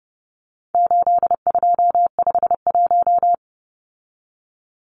Morse code, heard as a keyed tone, about 30 wpm, 700 Hz. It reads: 8251
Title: Numbers - 4-Digit 30wpm